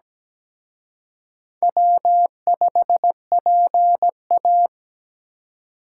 Morse code, heard as a keyed tone, about 20 wpm, 700 Hz. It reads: W5PA
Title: Speed Racing - USA 1x2 Call Signs 17wpm